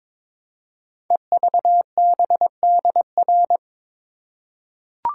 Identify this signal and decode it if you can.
Morse code, heard as a keyed tone, about 22 wpm, 700 Hz. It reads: EVBDR
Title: Code Groups - 5 Letter:Numbers - 22wpm